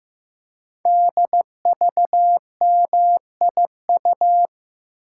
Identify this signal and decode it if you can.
Morse code, heard as a keyed tone, about 15 wpm, 700 Hz. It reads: DVMIU
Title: Code Groups - 5 Letters - 15wpm